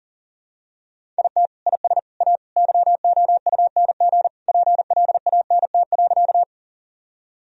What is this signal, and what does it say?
Morse code, heard as a keyed tone, about 37 wpm, 700 Hz. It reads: IT IS A YOUNG PLANT.
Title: Sentences from Top 1000 Words - Review Part 1 - 40wpm